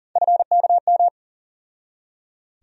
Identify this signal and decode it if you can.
Morse code, heard as a keyed tone, about 40 wpm, 700 Hz. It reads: FKM
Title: USA Call Sign Postfixes - 3 Letters 40wpm